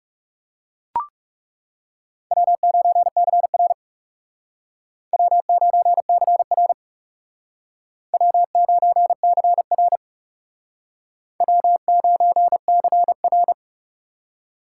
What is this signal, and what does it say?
Morse code, heard as a keyed tone, about 36 wpm, 700 Hz. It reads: W9CR W9CR W9CR W9CR
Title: Speed Racing - USA 1x2 Call Signs 30wpm